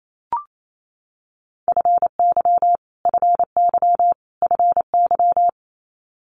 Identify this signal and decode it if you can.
Morse code, heard as a keyed tone, about 28 wpm, 700 Hz. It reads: FY FY FY
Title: Multi-letter Phonemes Repeated 3x 28wpm